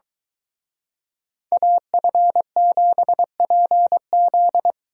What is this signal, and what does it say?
Morse code, heard as a keyed tone, about 22 wpm, 700 Hz. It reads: AF7PZ